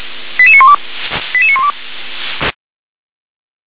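Five-tone selective call (ZVEI), heard as quick runs of five tones, tones of about 70 ms. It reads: 88912 88912
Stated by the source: ZVEI five-tone call 88152 received by SDR.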